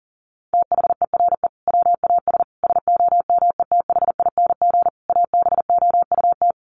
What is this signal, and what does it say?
Morse code, heard as a keyed tone, about 40 wpm, 700 Hz. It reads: THERE WAS SOMETHING ABOUT
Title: Sentences from Top 1500 Words 40wpm